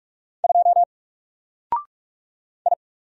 Morse code, heard as a keyed tone, about 44 wpm, 700 Hz. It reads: 2 I
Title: Single Character (Letter:Number:Punctuation:Prosign) Alt 45wpm